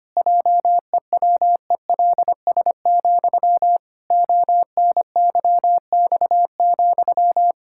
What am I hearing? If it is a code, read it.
Morse code, heard as a keyed tone, about 25 wpm, 700 Hz. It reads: JEWELS, ONYX,
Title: Pangram Challenge v2 - 25wpm